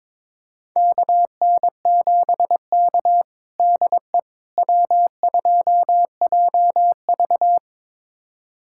Morse code, heard as a keyed tone, about 22 wpm, 700 Hz. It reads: KN7K DE W2JV
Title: USA Callsign DE Callsign - 4 Characters - 22wpm